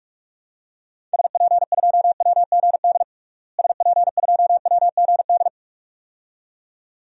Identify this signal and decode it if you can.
Morse code, heard as a keyed tone, about 46 wpm, 700 Hz. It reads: SP2WGD SP2WGD
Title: DXCC-100 Call Signs - 6 Characters - Repeated 2x - 45wpm